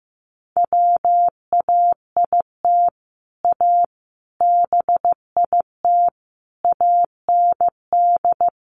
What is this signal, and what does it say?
Morse code, heard as a keyed tone, about 15 wpm, 700 Hz. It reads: WAIT A BIT AND
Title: Sentences from Top 900 Words 15wpm